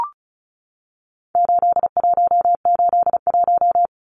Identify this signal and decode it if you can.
Morse code, heard as a keyed tone, about 34 wpm, 700 Hz. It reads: 8181
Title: Numbers - 4-Digit 35wpm